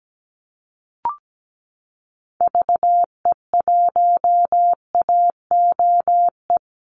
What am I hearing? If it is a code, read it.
Morse code, heard as a keyed tone, about 17 wpm, 700 Hz. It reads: VE1AOE
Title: Call Signs Repeated 3x - 17wpm